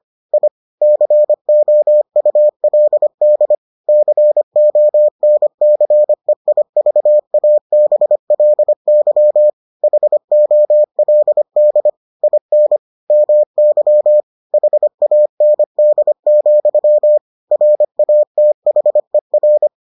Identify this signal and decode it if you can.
Morse code, heard as a keyed tone, about 25 wpm, 600 Hz. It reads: I COULD CONCEIVABLY HOLD IN MY HAND, RATHER